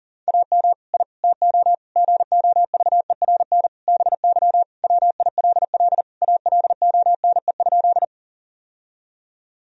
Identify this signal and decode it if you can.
Morse code, heard as a keyed tone, about 40 wpm, 700 Hz. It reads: AM I TO GOVERN BY WILL ALONE?
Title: Sentences from Top 1000 Words - Review Part 2 - 40wpm